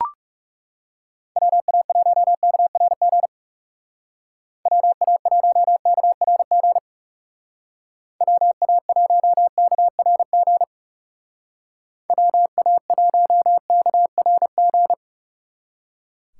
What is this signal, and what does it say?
Morse code, heard as a keyed tone, about 36 wpm, 700 Hz. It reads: WA1KRG WA1KRG WA1KRG WA1KRG
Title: Speed Racing - USA 2x3 Call Signs 30wpm